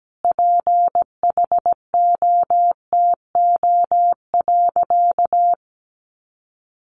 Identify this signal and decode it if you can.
Morse code, heard as a keyed tone, about 17 wpm, 700 Hz. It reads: PHOTO.
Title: Sentences from Top 1200 Words 17wpm